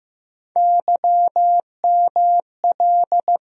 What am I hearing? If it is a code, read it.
Morse code, heard as a keyed tone, about 15 wpm, 700 Hz. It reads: YML
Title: USA Call Sign Postfixes - 3 Letters 15wpm